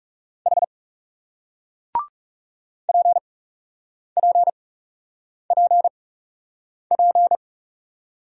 Morse code, heard as a keyed tone, about 40 wpm, spaced as for 8 wpm, 700 Hz. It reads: H PPPP